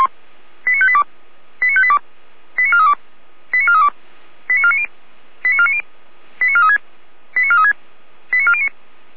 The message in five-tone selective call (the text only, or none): 89582 89582 89432 89432 89490 89490 89436 89436 89409